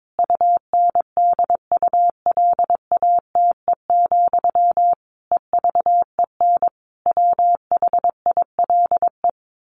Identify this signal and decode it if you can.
Morse code, heard as a keyed tone, about 22 wpm, 700 Hz. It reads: UNDULATE, EVEN WHILE
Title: A Sound Of Thunder 22wpm